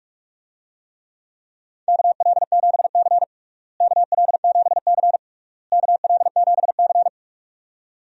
Morse code, heard as a keyed tone, about 42 wpm, 700 Hz. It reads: KL7C KL7C KL7C
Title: DXCC-100 Call Signs - 4 Characters - Repeated 3x - 45wpm